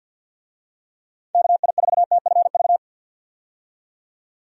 Morse code, heard as a keyed tone, about 49 wpm, 700 Hz. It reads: KI4TFV
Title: USA Call Signs - 6 Characters - 50wpm